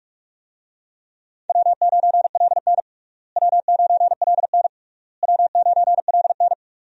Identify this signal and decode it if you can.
Morse code, heard as a keyed tone, about 46 wpm, 700 Hz. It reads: W9LN W9LN W9LN
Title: USA Call Signs - 4 Characters - Repeated 3x - 45wpm